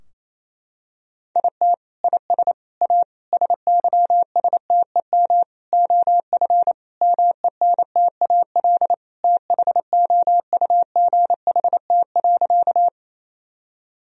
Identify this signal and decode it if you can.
Morse code, heard as a keyed tone, about 28 wpm, 700 Hz. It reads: IT IS A SYSTEM OF MENTAL THOUGHT.
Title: Sentences from Top 1000 Words - Review Part 4 - 28wpm